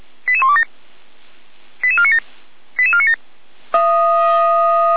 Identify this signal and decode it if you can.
Five-tone selective call (ZVEI), heard as quick runs of five tones, tones of about 70 ms. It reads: 88127 88497 88497